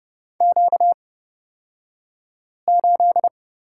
Morse code, heard as a keyed tone, about 30 wpm, spaced as for 5 wpm, 700 Hz. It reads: Q8
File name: course / 30:10wpm / 235 Lesson 38d - Letter q - Two Characters 30wpm